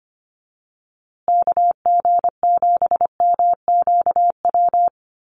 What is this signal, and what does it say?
Morse code, heard as a keyed tone, about 25 wpm, 700 Hz. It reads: KG7MQW
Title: USA Call Signs Repeated 2x 25wpm